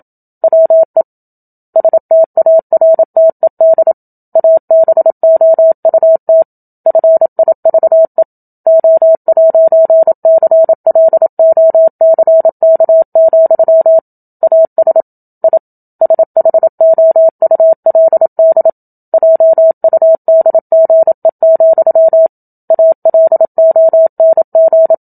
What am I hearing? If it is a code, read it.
Morse code, heard as a keyed tone, about 27 wpm, 650 Hz. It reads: WE STARTED ABOUT FIVE O'CLOCK, AS I SHOULD JUDGE, ALONG
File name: Book Two / 01-Chapter 1 - 27 WPM